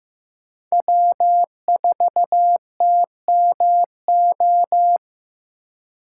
Morse code, heard as a keyed tone, about 16 wpm, 700 Hz. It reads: W4TMO
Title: K1USN SST QSO Practice - 15wpm